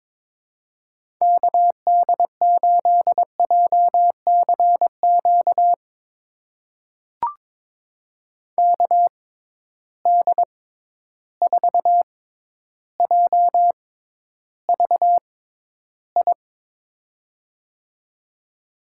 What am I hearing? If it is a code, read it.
Morse code, heard as a keyed tone, about 22 wpm, spaced as for 11 wpm, 700 Hz. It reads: KD8JCQ KD4JVI
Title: USA Call Signs - 6 Characters - 6x Character Spacing - 22wpm